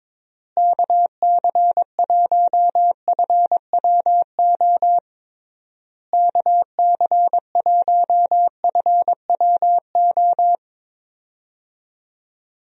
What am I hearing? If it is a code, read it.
Morse code, heard as a keyed tone, about 22 wpm, 700 Hz. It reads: KC1FWO KC1FWO
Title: USA Call Signs - 6 Characters - Repeated 3x - 22wpm